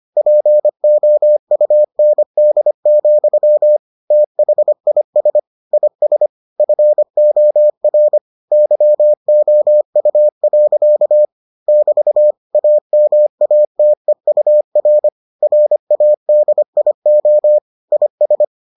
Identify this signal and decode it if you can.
Morse code, heard as a keyed tone, about 25 wpm, 600 Hz. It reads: POUND, THIS IS FOR YOU. <BT> AMATEUR RADIO IS